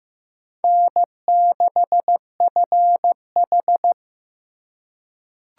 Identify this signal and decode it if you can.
Morse code, heard as a keyed tone, about 15 wpm, 700 Hz. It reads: N6FH
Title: USA Call Signs - 4 Characters - 4x Character Spacing - 15wpm